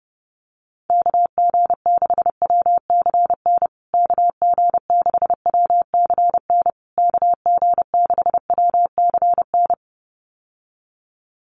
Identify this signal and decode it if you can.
Morse code, heard as a keyed tone, about 28 wpm, 700 Hz. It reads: KG6WCN KG6WCN KG6WCN
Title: USA Call Signs Repeated 3x 30wpm